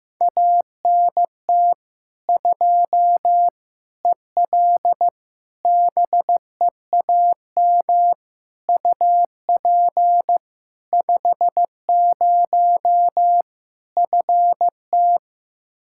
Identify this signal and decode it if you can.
Morse code, heard as a keyed tone, about 15 wpm, 700 Hz. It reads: ANT 2 EL BEAM UP 50 FT
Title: Practice QSO - Exchange 2 - Info - 15wpm